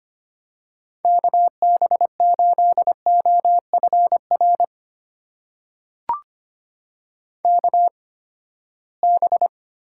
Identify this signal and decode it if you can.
Morse code, heard as a keyed tone, about 25 wpm, spaced as for 17 wpm, 700 Hz. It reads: KB8OFR KB
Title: USA Call Signs - 6 Characters - 8x Character Spacing - 25wpm